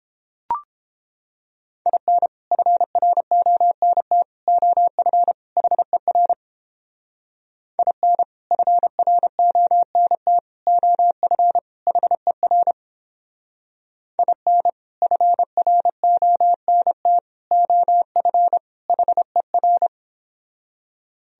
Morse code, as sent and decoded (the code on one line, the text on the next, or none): .. -. / ..-. .-. --- -. - / --- ..-. / .... .-. / .. -. / ..-. .-. --- -. - / --- ..-. / .... .-. / .. -. / ..-. .-. --- -. - / --- ..-. / .... .-.
IN FRONT OF HER IN FRONT OF HER IN FRONT OF HER